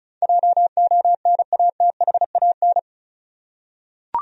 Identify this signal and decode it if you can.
Morse code, heard as a keyed tone, about 34 wpm, 700 Hz. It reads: JONATHAN
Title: USA Common Names - 35wpm